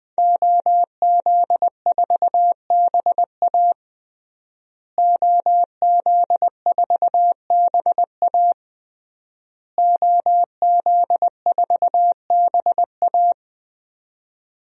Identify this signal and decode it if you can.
Morse code, heard as a keyed tone, about 20 wpm, 700 Hz. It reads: OZ4BA OZ4BA OZ4BA
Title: DXCC-100 Call Signs - 5 Characters - Repeated 3x - 20wpm